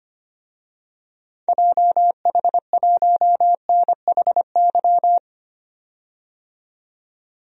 Morse code, heard as a keyed tone, about 25 wpm, 700 Hz. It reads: JH1NHY
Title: DXCC-100 Call Signs - 6 Characters - 25wpm